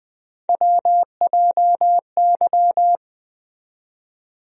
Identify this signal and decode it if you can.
Morse code, heard as a keyed tone, about 20 wpm, 700 Hz. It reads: WJY
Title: USA Call Sign Postfixes - 3 Letters 20wpm